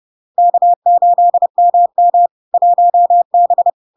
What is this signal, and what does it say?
Morse code, heard as a keyed tone, about 30 wpm, 700 Hz. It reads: K8MM 1B
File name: LARK-FD-2022-Log-in-CW-30wpm-700-Hz